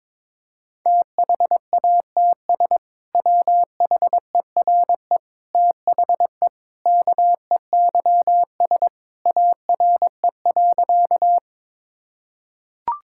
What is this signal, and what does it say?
Morse code, heard as a keyed tone, about 21 wpm, 700 Hz. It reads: THATS WHERE THE KEYS ARE.
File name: practice / 22wpm / Sentences with 2-Word Phrases - 22wpm